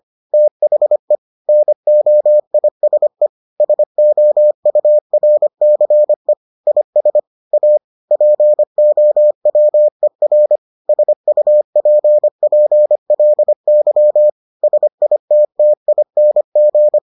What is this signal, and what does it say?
Morse code, heard as a keyed tone, about 25 wpm, 600 Hz. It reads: THE NOISE SOURCE IS A POWER SUPPLY SITTING